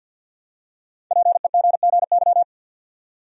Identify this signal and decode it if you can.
Morse code, heard as a keyed tone, about 50 wpm, 700 Hz. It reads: PEGGY
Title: USA Common Names - 50wpm